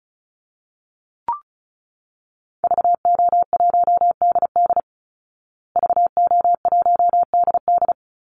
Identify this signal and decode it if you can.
Morse code, heard as a keyed tone, about 34 wpm, 700 Hz. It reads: VO1DD VO1DD
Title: Call Signs Repeated 2x - 35wpm